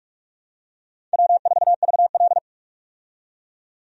Morse code, heard as a keyed tone, about 45 wpm, 700 Hz. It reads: W4VL
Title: USA Call Signs - 4 Characters - 45wpm